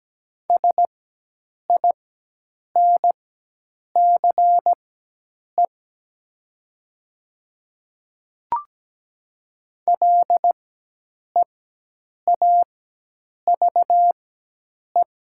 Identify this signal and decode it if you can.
Morse code, heard as a keyed tone, about 17 wpm, spaced as for 8 wpm, 700 Hz. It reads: SINCE LEAVE
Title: 4x Character Spacing - Top 500 Words - 17wpm